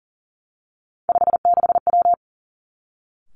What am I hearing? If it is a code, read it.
Morse code, heard as a keyed tone, about 42 wpm, 700 Hz. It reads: 56W
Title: Three Challenging Letter-Number Combinations 40wpm